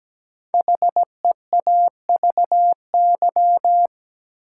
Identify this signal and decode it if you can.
Morse code, heard as a keyed tone, about 17 wpm, 700 Hz. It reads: HEAVY
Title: Sentences from Top 1000 Words - Review Part 6 - New Word Drill - 17wpm